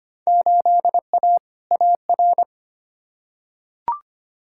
Morse code, heard as a keyed tone, about 25 wpm, 700 Hz. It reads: 8A AR
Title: Field Day Exchanges - 25wpm